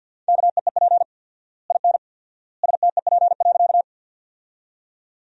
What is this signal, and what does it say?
Morse code, heard as a keyed tone, about 50 wpm, 700 Hz. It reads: KEEP IN STEP.
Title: 4x Word-Spacing - Sentences from Top 400 Words - 50wpm